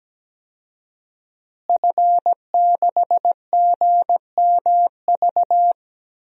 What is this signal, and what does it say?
Morse code, heard as a keyed tone, about 17 wpm, 700 Hz. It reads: F6GMV
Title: DXCC-100 Call Signs - 5 Characters - Repeated 3x - 17wpm